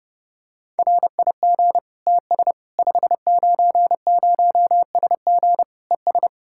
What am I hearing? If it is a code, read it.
Morse code, heard as a keyed tone, about 29 wpm, 700 Hz. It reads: RIG TS 590SG ES
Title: Practice QSO - Exchange 2 - Info - 30wpm